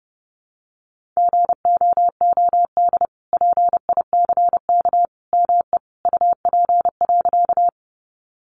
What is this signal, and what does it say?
Morse code, heard as a keyed tone, about 30 wpm, 700 Hz. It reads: GOOD PICK ME UP.
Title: Sentences from Top 1000 Words 30wpm